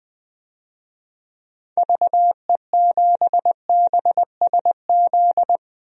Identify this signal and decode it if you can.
Morse code, heard as a keyed tone, about 20 wpm, 700 Hz. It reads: VE7BSZ
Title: DXCC-100 Call Signs - 6 Characters - 20wpm